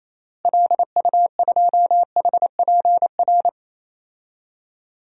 Morse code, heard as a keyed tone, about 28 wpm, 700 Hz. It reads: LU2HPR
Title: DXCC-100 Call Signs - 6 Characters - Repeated 2x - 28wpm